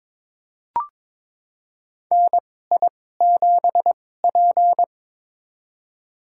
Morse code, heard as a keyed tone, about 22 wpm, spaced as for 16 wpm, 700 Hz. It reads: NI7P
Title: USA Call Signs - 2x Character Spacing - 22wpm